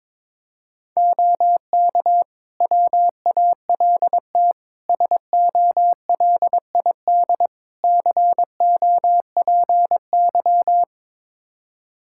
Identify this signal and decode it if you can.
Morse code, heard as a keyed tone, about 22 wpm, 700 Hz. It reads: OK WALT SOLID COPY
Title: Practice QSO - Exchange 3 - More Info - 22wpm